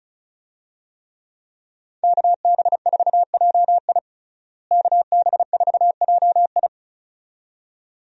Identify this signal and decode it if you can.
Morse code, heard as a keyed tone, about 36 wpm, 700 Hz. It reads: KB4JI KB4JI
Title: USA Call Signs - 5 Characters - Repeated 2x - 35wpm